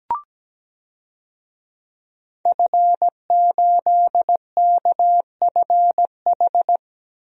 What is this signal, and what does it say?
Morse code, heard as a keyed tone, about 17 wpm, 700 Hz. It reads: F8KFH
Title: DXCC-100 Call Signs - 5 Characters - Repeated 2x - 17wpm